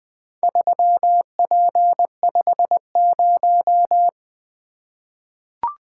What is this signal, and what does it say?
Morse code, heard as a keyed tone, about 20 wpm, 700 Hz. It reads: 3P50
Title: Four Letter-Number Combinations - 20wpm